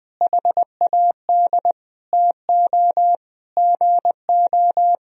Morse code, heard as a keyed tone, about 19 wpm, 700 Hz. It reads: HAD TO GO
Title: Sentences from Top 1000 Words - Review Part 1 - 20wpm